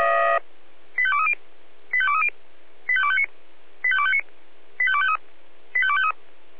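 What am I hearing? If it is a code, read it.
Five-tone selective call (ZVEI), heard as quick runs of five tones, tones of about 70 ms. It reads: 85230 85230 85259 85259 85253 85253
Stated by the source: Five-tone ZVEI selective call received by SDR.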